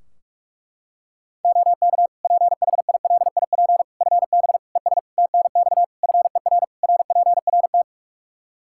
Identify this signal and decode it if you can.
Morse code, heard as a keyed tone, about 41 wpm, 700 Hz. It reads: OK PHILIP FB ES TNX FER RPRT
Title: K1USN SST QSO Practice 45wpm